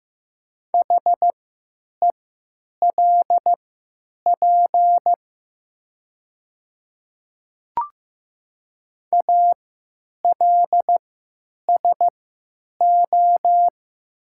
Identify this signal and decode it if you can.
Morse code, heard as a keyed tone, about 15 wpm, spaced as for 9 wpm, 700 Hz. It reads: HELP ALSO
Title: 3x Character Spacing - Top 200 Words - 15wpm